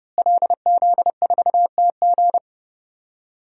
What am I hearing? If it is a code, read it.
Morse code, heard as a keyed tone, about 30 wpm, 700 Hz. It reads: LZ4TG
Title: DXCC-100 Call Signs - 5 Characters - Repeated 2x - 30wpm